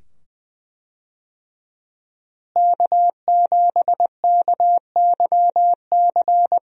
Morse code, heard as a keyed tone, about 21 wpm, 700 Hz. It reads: K7KYC
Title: USA Call Signs - 5 Characters - Repeated 2x - 20wpm